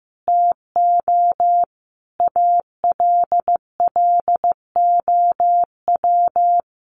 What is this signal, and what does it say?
Morse code, heard as a keyed tone, about 15 wpm, 700 Hz. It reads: TO ALLOW
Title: Sentences from Top 1300 Words 15wpm